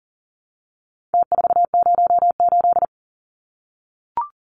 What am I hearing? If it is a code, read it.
Morse code, heard as a keyed tone, about 40 wpm, 700 Hz. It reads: T408
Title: Numbers - Contest Style - 4-Digit 40wpm